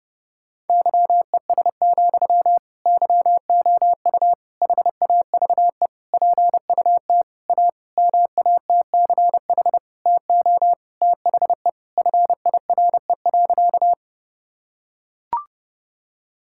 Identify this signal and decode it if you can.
Morse code, heard as a keyed tone, about 32 wpm, 700 Hz. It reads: YES, YOU HAVE PUT A MATCH TO THE FIRE.
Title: Sentences from Top 1600 Words 30wpm